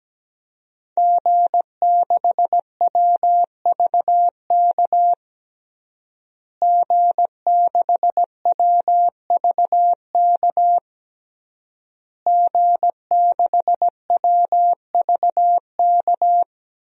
Morse code, heard as a keyed tone, about 17 wpm, 700 Hz. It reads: G6WVK G6WVK G6WVK
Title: DXCC-100 Call Signs - 5 Characters - Repeated 3x - 17wpm